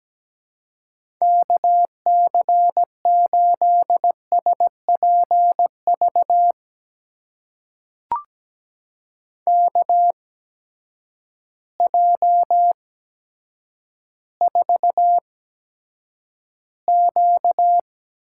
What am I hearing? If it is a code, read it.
Morse code, heard as a keyed tone, about 17 wpm, spaced as for 9 wpm, 700 Hz. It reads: KC8SPV KJ4Q
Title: USA Call Signs - 6 Characters - 8x Character Spacing - 17wpm